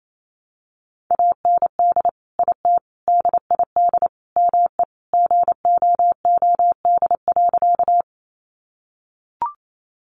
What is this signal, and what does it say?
Morse code, heard as a keyed tone, about 28 wpm, 700 Hz. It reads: AND IT DID ME GOOD.
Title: Sentences from Top 200 Words 28wpm